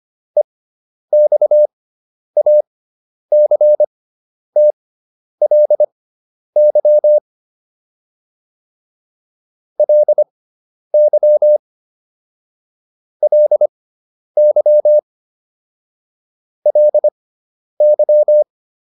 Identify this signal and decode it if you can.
Morse code, heard as a keyed tone, about 25 wpm, spaced as for 10 wpm, 600 Hz. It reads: EXACTLY LY LY LY